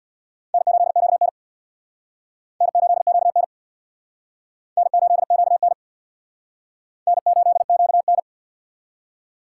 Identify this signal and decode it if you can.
Morse code, heard as a keyed tone, about 62 wpm, 700 Hz. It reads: N8QN N8QN N8QN N8QN
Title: Speed Racing - USA Call Signs 50wpm